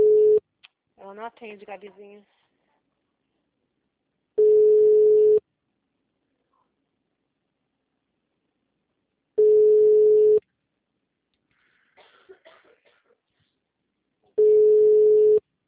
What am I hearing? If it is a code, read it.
Ringback tone (Europe, 425 Hz), with speech.
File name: out- -2049- - - .22225